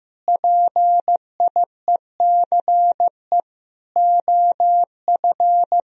Morse code, heard as a keyed tone, about 16 wpm, 700 Hz. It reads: PIECE OF
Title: Sentences from Top 1000 Words - Review Part 2 - 15wpm